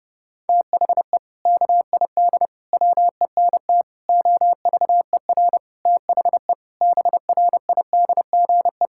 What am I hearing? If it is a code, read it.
Morse code, heard as a keyed tone, about 30 wpm, 700 Hz. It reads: THE KID WENT OVER THE BRIDGE
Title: Sentences from Top 1600 Words 30wpm